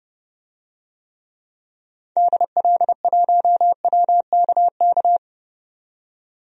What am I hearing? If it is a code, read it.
Morse code, heard as a keyed tone, about 32 wpm, 690 Hz. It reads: DL1WKK
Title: DXCC-100 Call Signs - 6 Characters - 30wpm